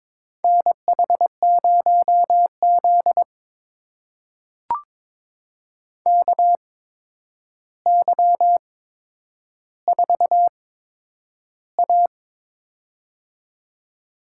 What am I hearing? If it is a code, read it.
Morse code, heard as a keyed tone, about 22 wpm, spaced as for 9 wpm, 700 Hz. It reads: NH0Z KY4A
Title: USA Call Signs - 4 Characters - 8x Character Spacing - 22wpm